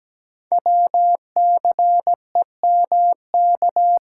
Morse code, heard as a keyed tone, about 17 wpm, 700 Hz. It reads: WCEMK
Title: Code Groups - 5 Letters - 17wpm